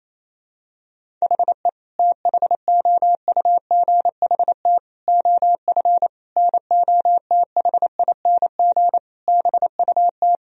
Morse code, heard as a keyed tone, about 28 wpm, 700 Hz. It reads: HE THOUGHT OF NOTHING BUT
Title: Sentences from Top 1000 Words - Review Part 4 - 28wpm